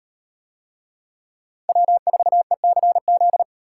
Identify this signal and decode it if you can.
Morse code, heard as a keyed tone, about 36 wpm, 700 Hz. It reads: W4ECZ